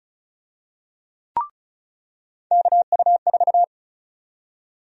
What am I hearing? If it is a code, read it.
Morse code, heard as a keyed tone, about 36 wpm, 700 Hz. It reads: KU4
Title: USA Call Sign Prefixes - 35wpm